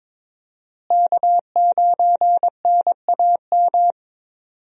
Morse code, heard as a keyed tone, about 22 wpm, 700 Hz. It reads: K9NAM
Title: USA Call Signs - 5 Characters - Repeated 3x - 22wpm